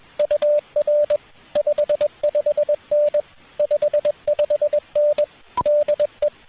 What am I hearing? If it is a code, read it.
Morse code, heard as a keyed tone, about 21 wpm, 590 Hz, with radio noise. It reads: UR 55N 55N DE